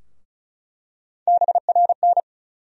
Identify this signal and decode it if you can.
Morse code, heard as a keyed tone, about 35 wpm, 700 Hz. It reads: BRN
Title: USA Call Sign Postfixes - 3 Letters 35wpm